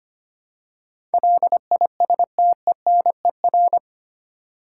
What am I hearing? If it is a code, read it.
Morse code, heard as a keyed tone, about 25 wpm, 700 Hz. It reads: LISTENER
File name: Eight Letter Words - 25wpm